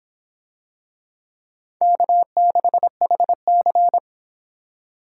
Morse code, heard as a keyed tone, about 26 wpm, 700 Hz. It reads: K6HC